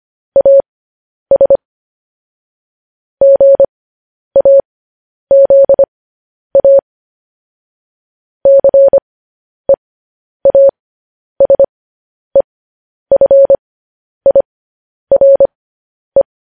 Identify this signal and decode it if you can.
Morse code, heard as a keyed tone, about 25 wpm, spaced as for 10 wpm, 550 Hz. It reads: AS GAZA CEASEFIRE
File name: podcast / News headlines-1x-25x10